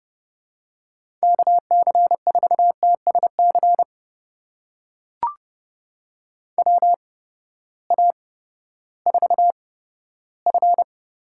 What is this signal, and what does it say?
Morse code, heard as a keyed tone, about 30 wpm, spaced as for 15 wpm, 700 Hz. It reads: KC4TSC WA4F
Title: USA Call Signs - 6 Characters - 8x Character Spacing - 30wpm